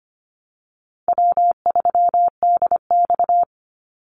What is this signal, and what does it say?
Morse code, heard as a keyed tone, about 25 wpm, 700 Hz. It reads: W3DX
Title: NA SOTA Activator Callsigns 25wpm